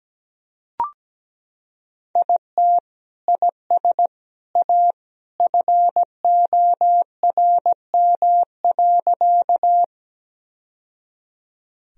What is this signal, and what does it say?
Morse code, heard as a keyed tone, about 17 wpm, 700 Hz. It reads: IT IS A FORM.
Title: Sentences from Top 1000 Words - Review Part 1 - 17wpm